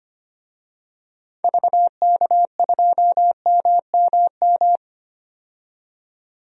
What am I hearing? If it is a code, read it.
Morse code, heard as a keyed tone, about 25 wpm, 700 Hz. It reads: VK2MMM
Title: DXCC-100 Call Signs - 6 Characters - 25wpm